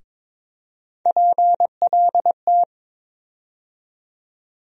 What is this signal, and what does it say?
Morse code, heard as a keyed tone, about 21 wpm, 700 Hz. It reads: PLT
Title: USA Call Sign Postfixes - 3 Letters 22wpm